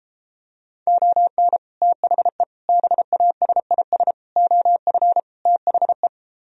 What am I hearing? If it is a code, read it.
Morse code, heard as a keyed tone, about 33 wpm, 700 Hz. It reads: ON THE BASIS OF THE